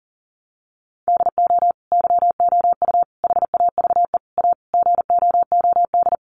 Morse code, heard as a keyed tone, about 37 wpm, 700 Hz. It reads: DO YOU HAVE A GOOD
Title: Sentences from Top 1300 Words 40wpm